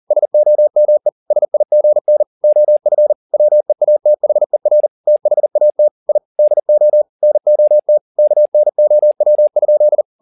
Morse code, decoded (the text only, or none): SOME SIGN OF WEATHER THAT I DO NOT KNOW?